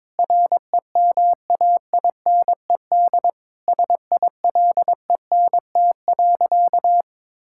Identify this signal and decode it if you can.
Morse code, heard as a keyed tone, about 22 wpm, 700 Hz. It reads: REMAINED SILENT.
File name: Sentences with 7-Word Phrases - 22wpm